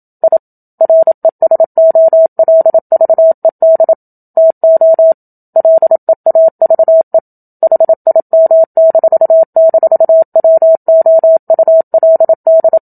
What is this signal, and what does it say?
Morse code, heard as a keyed tone, about 27 wpm, 650 Hz. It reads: I RESOLVED TO LEAVE HIM--WOULD
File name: Book Two / 01-Chapter 1 - 27 WPM